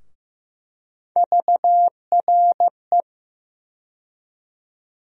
Morse code, heard as a keyed tone, about 15 wpm, 700 Hz. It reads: VRE
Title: USA Call Sign Postfixes - 3 Letters 15wpm